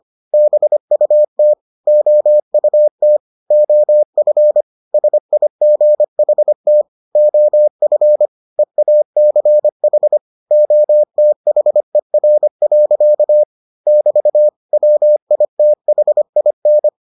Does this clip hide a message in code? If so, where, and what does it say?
Yes: Morse code from 0.3 to 16.9 s, heard as a keyed tone, about 25 wpm, 600 Hz. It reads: BUT OUT OF SIGHT OF EACH OTHER. <BT> WITHIN